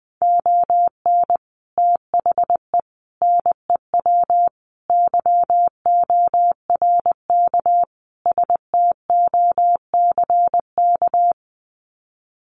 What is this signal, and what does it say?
Morse code, heard as a keyed tone, about 20 wpm, 700 Hz. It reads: ON THE NEW YORK STOCK
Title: Sets of 5 Words Encore 20wpm